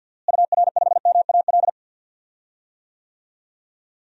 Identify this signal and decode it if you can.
Morse code, heard as a keyed tone, about 48 wpm, 700 Hz. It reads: UR5MAL
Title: DXCC-100 Call Signs - 6 Characters - Repeated 3x - 50wpm